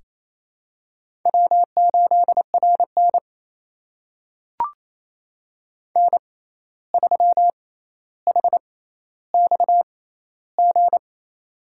Morse code, heard as a keyed tone, about 28 wpm, spaced as for 13 wpm, 700 Hz. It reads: W8RN N3HXG
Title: USA Call Signs - 6x Character Spacing - 28wpm